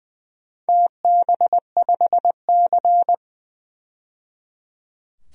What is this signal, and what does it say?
Morse code, heard as a keyed tone, about 20 wpm, 700 Hz. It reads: TB5C
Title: DXCC-100 Call Signs - 4 Characters - 20wpm